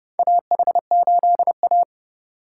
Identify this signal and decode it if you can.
Morse code, heard as a keyed tone, about 30 wpm, 700 Hz. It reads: AH8A
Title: USA Call Signs - 2x Character Spacing - 30wpm